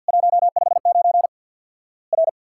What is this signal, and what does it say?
Morse code, heard as a keyed tone, about 50 wpm, 700 Hz. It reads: 159 R
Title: Worked All Europe DX Contest - QTC Exchange with Repeats - 50wpm